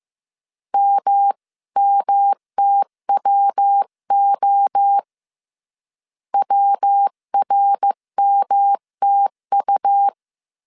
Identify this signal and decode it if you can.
Morse code, heard as a keyed tone, about 15 wpm, 800 Hz. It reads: MMTWO WRMTU
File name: CW / DET / 1 3 1500C S C 14wpm 2458